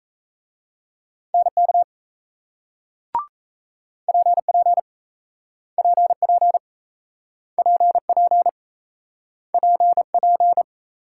Morse code, heard as a keyed tone, about 38 wpm, 700 Hz. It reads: NK PP PP PP PP